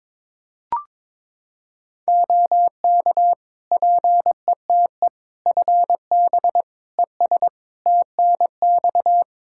Morse code, heard as a keyed tone, about 22 wpm, 700 Hz. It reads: OK PETE FB ES TNX
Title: Practice QSO - Exchange 2 - Info - 22wpm